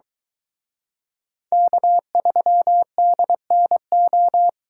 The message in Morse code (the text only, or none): K3DNO